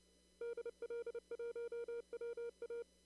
Morse code, heard as a keyed tone, about 29 wpm, 470 Hz. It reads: DL1WA
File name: CW / kt5j wpx cw-op-k5pi